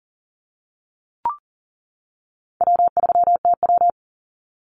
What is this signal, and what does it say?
Morse code, heard as a keyed tone, about 40 wpm, 700 Hz. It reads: W3TW
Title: Easy Call Signs - 40wpm